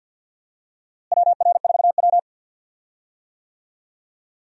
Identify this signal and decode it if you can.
Morse code, heard as a keyed tone, about 51 wpm, 700 Hz. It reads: WR4W
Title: USA Call Signs - 4 Characters - Repeated 3x - 50wpm